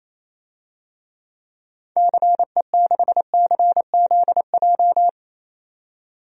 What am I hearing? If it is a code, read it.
Morse code, heard as a keyed tone, about 28 wpm, 700 Hz. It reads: CE6CZJ
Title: DXCC-100 Call Signs - 6 Characters - 28wpm